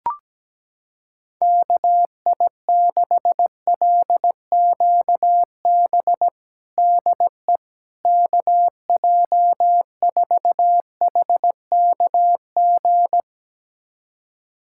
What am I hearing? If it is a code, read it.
Morse code, heard as a keyed tone, about 17 wpm, 700 Hz. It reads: KI6LQB DE KJ4HKG
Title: USA Callsign DE Callsign - 6 Characters - 17wpm